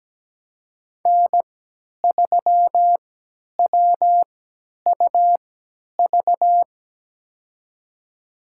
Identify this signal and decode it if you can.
Morse code, heard as a keyed tone, about 17 wpm, spaced as for 10 wpm, 700 Hz. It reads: N3WUV
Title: USA Call Signs - 5 Characters - 3x Character Spacing - 17wpm